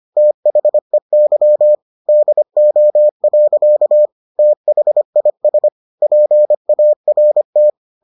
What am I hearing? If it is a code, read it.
Morse code, heard as a keyed tone, about 25 wpm, 600 Hz. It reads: THEY DO. THIS PART